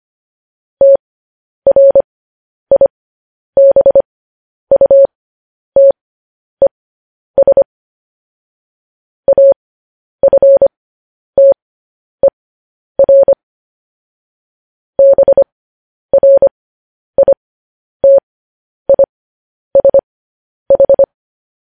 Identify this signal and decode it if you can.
Morse code, heard as a keyed tone, about 25 wpm, spaced as for 10 wpm, 550 Hz. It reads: TRIBUTES AFTER BRITISH